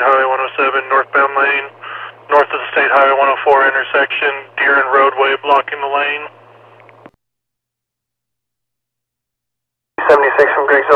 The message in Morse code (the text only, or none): none